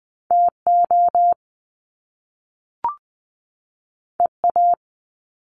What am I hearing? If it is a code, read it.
Morse code, heard as a keyed tone, about 20 wpm, 700 Hz. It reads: TO EA
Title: Top Two Letter Combinations in Words 20wpm